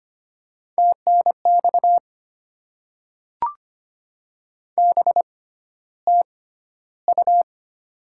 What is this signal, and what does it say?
Morse code, heard as a keyed tone, about 25 wpm, spaced as for 13 wpm, 700 Hz. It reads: TNX BTU
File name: Top 20 QSO Elements - 6x Character Spacing - 25wpm